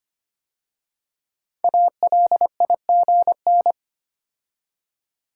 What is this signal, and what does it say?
Morse code, heard as a keyed tone, about 25 wpm, 700 Hz. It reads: ALIGN
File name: 5 Letter Words - First 3 Letters Given - 25wpm